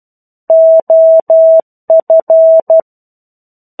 Morse code, heard as a keyed tone, about 12 wpm, 650 Hz. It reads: OF
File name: 01-Chapter-01 12 WPM